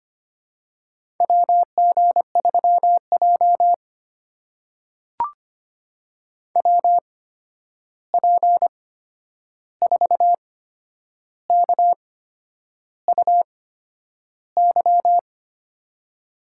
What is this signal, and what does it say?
Morse code, heard as a keyed tone, about 25 wpm, spaced as for 9 wpm, 700 Hz. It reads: WG3J WP4KUY
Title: USA Call Signs - 8x Character Spacing - 25wpm